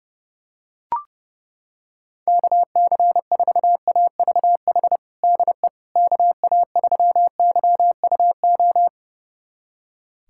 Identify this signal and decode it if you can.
Morse code, heard as a keyed tone, about 30 wpm, 700 Hz. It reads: KC4AVH DE KA3YUO
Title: USA Callsign DE Callsign - 6 Characters - 30wpm